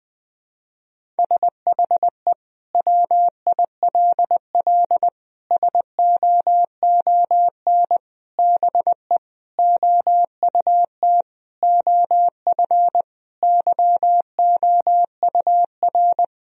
Morse code, heard as a keyed tone, about 20 wpm, 700 Hz. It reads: SHE WILL SOON BE OUT OF YOUR
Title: Sentences from Top 1000 Words - Review Part 2 - 20wpm